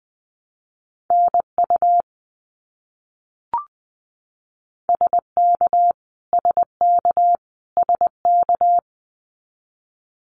Morse code, heard as a keyed tone, about 20 wpm, 700 Hz. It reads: NU SK SK SK
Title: Canadian Provinces and Territories - Repeated 3x - 20wpm